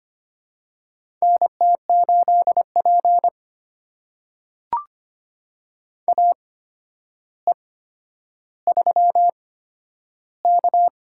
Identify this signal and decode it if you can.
Morse code, heard as a keyed tone, about 25 wpm, spaced as for 11 wpm, 700 Hz. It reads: NT8P AE3K
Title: USA Call Signs - 4 Characters - 8x Character Spacing - 25wpm